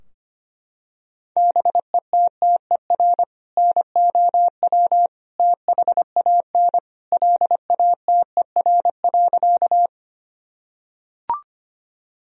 Morse code, heard as a keyed tone, about 25 wpm, 700 Hz. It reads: BETTER NOW THAN LATER.
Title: Sentences from Top 1000 Words - Review Part 4 - 25wpm